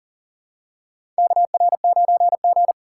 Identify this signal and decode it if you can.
Morse code, heard as a keyed tone, about 40 wpm, 700 Hz. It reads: KR9G
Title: USA Call Signs - 4 Characters - 6x Character Spacing - 40wpm